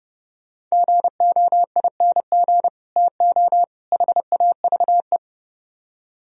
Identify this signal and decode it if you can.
Morse code, heard as a keyed tone, about 36 wpm, 700 Hz. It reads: GOING TO HAVE
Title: Speed-Racing - Sets of 3 Words 30wpm